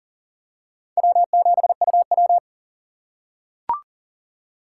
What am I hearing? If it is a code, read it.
Morse code, heard as a keyed tone, about 40 wpm, 700 Hz. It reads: W7UW
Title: USA Call Signs - 4 Characters - 4x Character Spacing - 40wpm